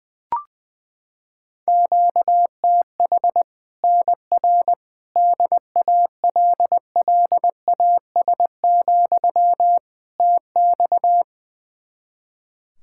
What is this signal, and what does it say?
Morse code, heard as a keyed tone, about 20 wpm, 700 Hz. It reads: QTH NR DALLAS, TX
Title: Practice QSO - Exchange 1 - Beginning - 20wpm